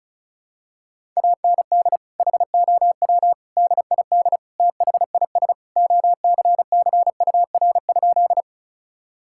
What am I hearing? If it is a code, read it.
Morse code, heard as a keyed tone, about 36 wpm, 700 Hz. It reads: AND HOW DID THIS OCCUR?
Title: Sentences from Top 1000 Words - Review Part 6 - 35wpm